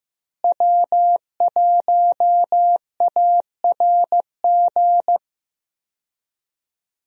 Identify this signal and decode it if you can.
Morse code, heard as a keyed tone, about 15 wpm, 700 Hz. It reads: W1ARG
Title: USA Call Signs - 5 Characters - Repeated 2x - 15wpm